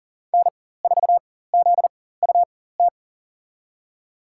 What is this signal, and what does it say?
Morse code, heard as a keyed tone, about 38 wpm, spaced as for 18 wpm, 700 Hz. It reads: N4ZUT
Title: USA Call Signs - 5 Characters - 4x Character Spacing - 40wpm